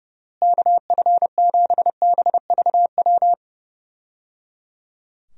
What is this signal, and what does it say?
Morse code, heard as a keyed tone, about 29 wpm, 700 Hz. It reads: KF7BVW
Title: USA Call Signs - 6 Characters - Repeated 3x - 30wpm